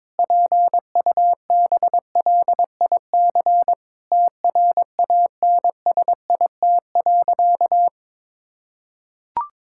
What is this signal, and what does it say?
Morse code, heard as a keyed tone, about 22 wpm, 700 Hz. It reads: PUBLIC TRANSIT.
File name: Difficult Sentences with Hints - 22wpm